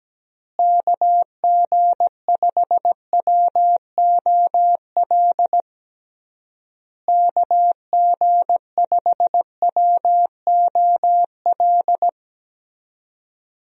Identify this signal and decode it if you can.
Morse code, heard as a keyed tone, about 17 wpm, 700 Hz. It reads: KG5WOL KG5WOL
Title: USA Call Signs - 6 Characters - Repeated 3x - 17wpm